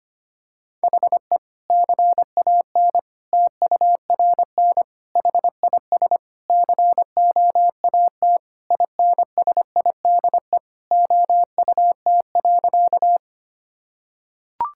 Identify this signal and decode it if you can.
Morse code, heard as a keyed tone, about 25 wpm, 700 Hz. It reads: HE CAN TURN HIS COAT INSIDE OUT.
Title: Sentences from Top 1600 Words 25wpm